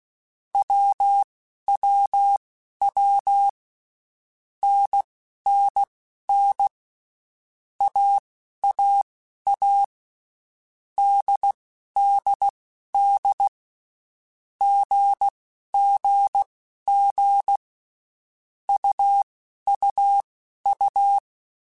Morse code, heard as a keyed tone, about 16 wpm, spaced as for 11 wpm, 800 Hz. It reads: WWW NNN AAA DDD GGG UUU
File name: Basic Letter Groups By7Groups / AWUNDG / AWUNDG@10